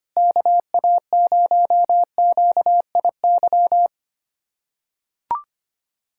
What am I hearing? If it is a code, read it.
Morse code, heard as a keyed tone, about 25 wpm, 700 Hz. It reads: KA0QIY
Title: USA Callsign DE Callsign - 6 Characters - 25wpm